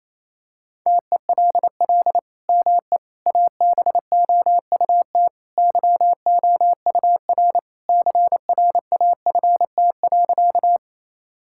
Morse code, heard as a keyed tone, about 28 wpm, 700 Hz. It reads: TELL ME ABOUT YOUR CRAFT.
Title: Sentences from Top 1000 Words - Review Part 6 - 28wpm